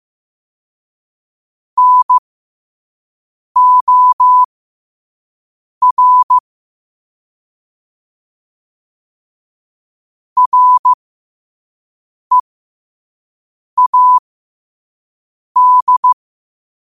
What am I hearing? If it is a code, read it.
Morse code, heard as a keyed tone, about 15 wpm, spaced as for 5 wpm, 1000 Hz. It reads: NOR READ
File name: qotdfiles / 5wpm